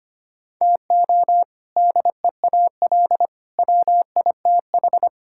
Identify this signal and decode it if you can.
Morse code, heard as a keyed tone, about 25 wpm, 700 Hz. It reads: TO DEAL WITH
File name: practice / 25wpm / Sentences with 3-Word Phrases - 25wpm